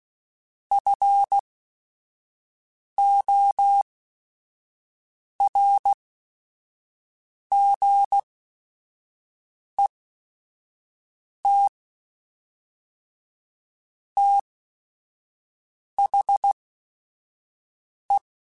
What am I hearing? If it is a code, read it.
Morse code, heard as a keyed tone, about 16 wpm, spaced as for 5 wpm, 800 Hz. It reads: FORGET THE
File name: Technical Subjects